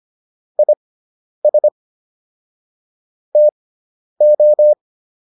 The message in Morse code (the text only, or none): IS TO